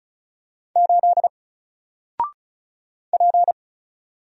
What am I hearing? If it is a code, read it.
Morse code, heard as a keyed tone, about 36 wpm, 700 Hz. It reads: 8 P
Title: Single Character (Letter:Number:Punctuation:Prosign) Alt 35wpm